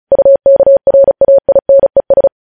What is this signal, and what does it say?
Morse code, heard as a keyed tone, about 36 wpm, 550 Hz. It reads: UKRAINES
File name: podcast / News headlines-1x-35x35